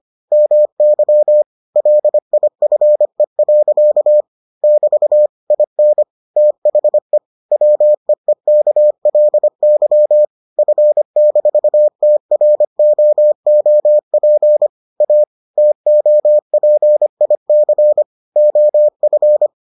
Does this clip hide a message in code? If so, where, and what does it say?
Yes: Morse code from 0.3 to 19.5 s, heard as a keyed tone, about 25 wpm, 600 Hz. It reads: MY LIFE. <BT> IN THE WEEKLY F-TROOP A TOPIC OF